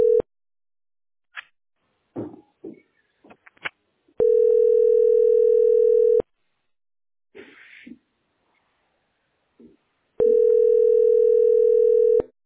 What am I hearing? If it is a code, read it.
Ringback tone (North America), 440 + 480 Hz.